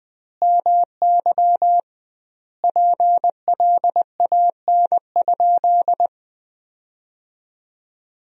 Morse code, heard as a keyed tone, about 20 wpm, 700 Hz. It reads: MY PLAN?
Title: 2x Word-Spacing - Sentences from Top 500 Words - 20wpm